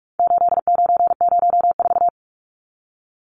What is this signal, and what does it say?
Morse code, heard as a keyed tone, about 44 wpm, 700 Hz. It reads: 8904
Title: Numbers - 4-Digit 45wpm